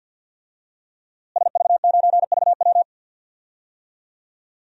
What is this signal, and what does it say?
Morse code, heard as a keyed tone, about 46 wpm, 700 Hz. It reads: SV9VW
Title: DXCC-100 Call Signs - 5 Characters - Repeated 3x - 50wpm